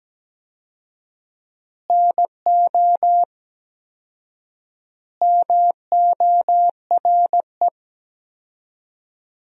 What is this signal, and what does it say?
Morse code, heard as a keyed tone, about 17 wpm, 700 Hz. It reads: NO MORE
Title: 4x Word-Spacing - Sentences from Top 200 Words - 17wpm